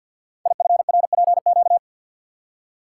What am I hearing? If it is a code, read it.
Morse code, heard as a keyed tone, about 53 wpm, 700 Hz. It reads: IFRPQ
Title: Code Groups - 5 Letter:Numbers - 50wpm